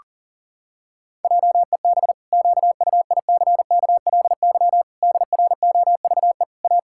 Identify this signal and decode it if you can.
Morse code, heard as a keyed tone, about 40 wpm, 700 Hz. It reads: JEB QUICKLY DROVE A
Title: Pangram Challenge v2 - 40wpm